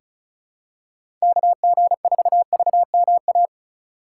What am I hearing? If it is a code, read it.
Morse code, heard as a keyed tone, about 36 wpm, 700 Hz. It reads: KG4VMA
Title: USA Call Signs - 6 Characters - 2x Character Spacing - 35wpm